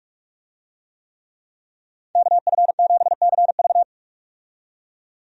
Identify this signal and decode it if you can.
Morse code, heard as a keyed tone, about 47 wpm, 700 Hz. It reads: KF7CV
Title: USA Call Signs - 5 Characters - 45wpm